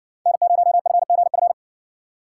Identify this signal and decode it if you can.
Morse code, heard as a keyed tone, about 60 wpm, 700 Hz. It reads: N0FGF